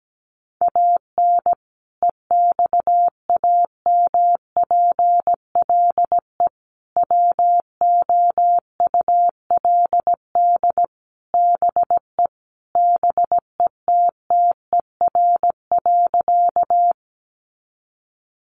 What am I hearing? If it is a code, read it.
Morse code, heard as a keyed tone, about 17 wpm, 700 Hz. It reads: AN EXAMPLE WOULD BE BETTER.
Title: Sentences from Top 400 Words 17wpm